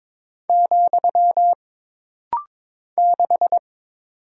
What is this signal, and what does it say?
Morse code, heard as a keyed tone, about 22 wpm, 700 Hz. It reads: , 6
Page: This rapid-fire timing is geared to push your ICR (Instant Character Recognition) skills.